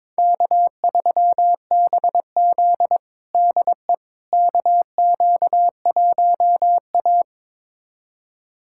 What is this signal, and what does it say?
Morse code, heard as a keyed tone, about 22 wpm, 700 Hz. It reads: K3BZ DE KQ1A
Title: USA Callsign DE Callsign - 4 Characters - 22wpm